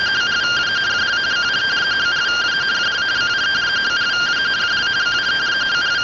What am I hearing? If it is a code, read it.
Two-tone teleprinter signal, about 170 Hz shift.
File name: findings / axial iodine 10mm 184.gen-16 rtty rx